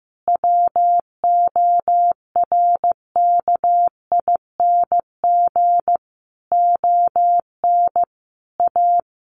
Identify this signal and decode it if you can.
Morse code, heard as a keyed tone, about 15 wpm, 700 Hz. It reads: WORKING ON A
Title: Sentences from Top 1400 Words 15wpm